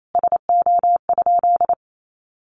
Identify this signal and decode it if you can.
Morse code, heard as a keyed tone, about 28 wpm, 700 Hz. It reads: SO?
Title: Sentences from Top 1300 Words 28wpm